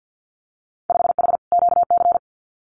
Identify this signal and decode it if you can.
Morse code, heard as a keyed tone, about 50 wpm, 700 Hz. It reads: 5H QC
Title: Winter Field Day Exchanges Encore 50wpm